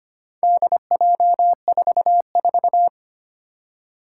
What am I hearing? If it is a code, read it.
Morse code, heard as a keyed tone, about 25 wpm, 700 Hz. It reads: DJ44
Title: Four Challenging Letter-Number Combinations 25wpm